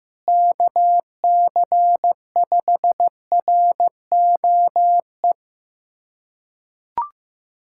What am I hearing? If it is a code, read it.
Morse code, heard as a keyed tone, about 15 wpm, 700 Hz. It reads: KC5ROE
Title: USA Callsign DE Callsign - 6 Characters - 15wpm